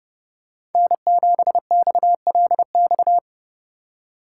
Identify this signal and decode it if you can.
Morse code, heard as a keyed tone, about 30 wpm, 700 Hz. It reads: N7XLX
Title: USA Call Signs - 5 Characters - 4x Character Spacing - 30wpm